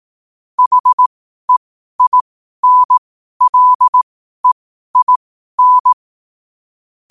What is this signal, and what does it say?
Morse code, heard as a keyed tone, about 18 wpm, spaced as for 12 wpm, 1000 Hz. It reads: HEINLEIN
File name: qotdfiles / 12wpm